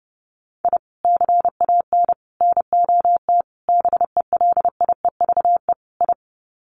Morse code, heard as a keyed tone, about 29 wpm, 700 Hz. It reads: I CAN NOT BELIEVE I
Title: Sets of 5 Words Encore2 30wpm